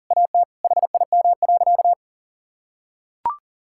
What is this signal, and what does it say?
Morse code, heard as a keyed tone, about 37 wpm, 700 Hz. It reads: AT HIM.
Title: Sentences from Top 1600 Words 40wpm